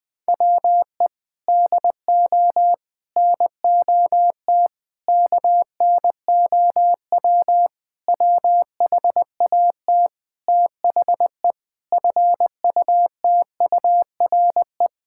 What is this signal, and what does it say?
Morse code, heard as a keyed tone, about 20 wpm, 700 Hz. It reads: WE DO NOT KNOW WHAT THE FUTURE
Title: Sentences with 5-Word Phrases - Spoken Beginning - 20wpm